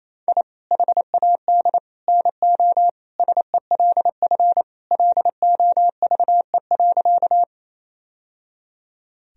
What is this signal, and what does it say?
Morse code, heard as a keyed tone, about 28 wpm, 700 Hz. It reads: I HAD NO SELF LOVE.
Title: Sentences from Top 1000 Words - Review Part 1 - 28wpm